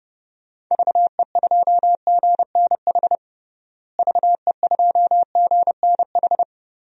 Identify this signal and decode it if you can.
Morse code, heard as a keyed tone, about 32 wpm, 700 Hz. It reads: VE2GNH VE2GNH
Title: DXCC-100 Call Signs - 6 Characters - Repeated 3x - 30wpm